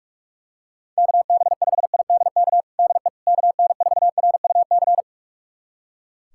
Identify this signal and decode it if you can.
Morse code, heard as a keyed tone, about 42 wpm, 700 Hz. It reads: KB5IDK DE KN4RUC
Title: USA Callsign DE Callsign - 6 Characters - 45wpm